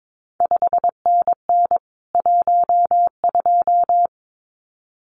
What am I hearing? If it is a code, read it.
Morse code, heard as a keyed tone, about 22 wpm, 700 Hz. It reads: 5NN 12
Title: CQ WW DX Contest Exchange Encore - 22wpm